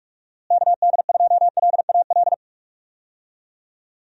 Morse code, heard as a keyed tone, about 43 wpm, 700 Hz. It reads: KD2LAL
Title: USA Call Signs - 6 Characters - Repeated 2x - 45wpm